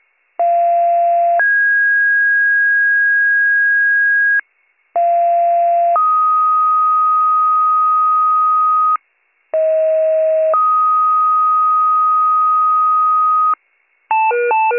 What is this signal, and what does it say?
Two-tone page, heard as a one-second tone, then a three-second tone, about 690 then 1650 Hz, about 690 then 1250 Hz, about 640 then 1250 Hz, with noise. It is heard from 0.4 to 13.5 s.